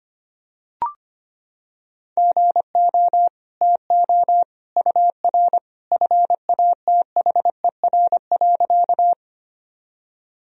Morse code, heard as a keyed tone, about 25 wpm, 700 Hz. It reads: GO TO UR FATHER.
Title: Sentences from Top 300 Words and Abbreviations 25wpm